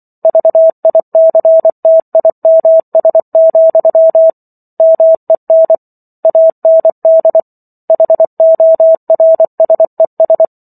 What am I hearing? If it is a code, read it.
Morse code, heard as a keyed tone, about 24 wpm, 650 Hz. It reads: VICTIMS, MEN AND HORSES